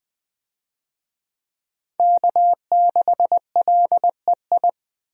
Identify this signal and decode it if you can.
Morse code, heard as a keyed tone, about 20 wpm, 700 Hz. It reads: K6LEI
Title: USA Call Signs - 5 Characters - 20wpm